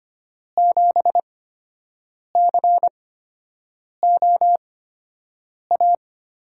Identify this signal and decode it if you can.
Morse code, heard as a keyed tone, about 25 wpm, spaced as for 7 wpm, 700 Hz. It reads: 7COA
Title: USA Call Signs - 6 Characters - 8x Character Spacing - 25wpm